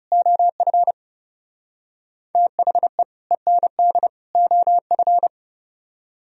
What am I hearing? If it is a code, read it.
Morse code, heard as a keyed tone, about 37 wpm, 700 Hz. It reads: OF THE END OF
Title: Speed-Racing - Sets of 3 Words 30wpm